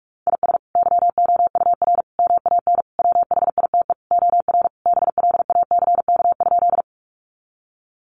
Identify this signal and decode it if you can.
Morse code, heard as a keyed tone, about 45 wpm, 700 Hz. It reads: IS YOUR MAN WHITE OR BLACK?
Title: Sentences from Top 400 Words 45wpm